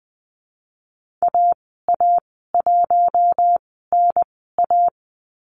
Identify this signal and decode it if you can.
Morse code, heard as a keyed tone, about 20 wpm, spaced as for 15 wpm, 700 Hz. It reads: AA1NA
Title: 042 Lesson 8g - Number 1 - Call Signs 20wpm